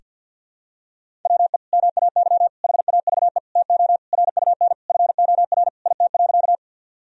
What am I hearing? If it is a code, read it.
Morse code, heard as a keyed tone, about 46 wpm, 700 Hz. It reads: WE MAY HAVE TO RUN FOR IT.
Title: Sentences from Top 1000 Words - Review Part 1 - 50wpm